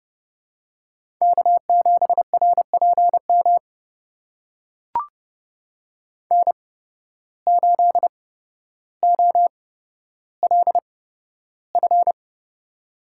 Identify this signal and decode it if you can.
Morse code, heard as a keyed tone, about 29 wpm, spaced as for 13 wpm, 700 Hz. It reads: K7RPM N8OLF
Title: USA Call Signs - 5 Characters - 8x Character Spacing - 30wpm